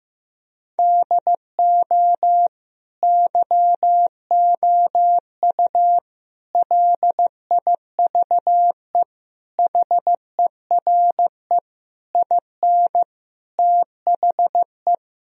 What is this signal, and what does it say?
Morse code, heard as a keyed tone, about 15 wpm, 700 Hz. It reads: DO YOU LIVE HERE IN THE
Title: Sentences from Top 1000 Words - Review Part 1 - 15wpm